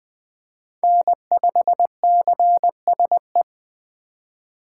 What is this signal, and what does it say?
Morse code, heard as a keyed tone, about 21 wpm, 700 Hz. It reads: N5CSE
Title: USA Call Signs - 5 Characters - 2x Character Spacing - 20wpm